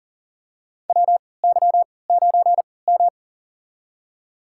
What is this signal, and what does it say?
Morse code, heard as a keyed tone, about 40 wpm, spaced as for 23 wpm, 700 Hz. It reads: WY9M
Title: USA Call Signs - 4 Characters - 3x Character Spacing - 40wpm